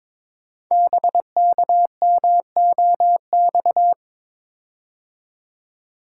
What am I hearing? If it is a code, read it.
Morse code, heard as a keyed tone, about 22 wpm, 700 Hz. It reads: BKMOX
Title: Code Groups - 5 Letters - 22wpm